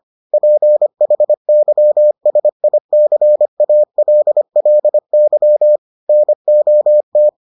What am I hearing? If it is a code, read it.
Morse code, heard as a keyed tone, about 25 wpm, 600 Hz. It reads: PHYSICALLY NOT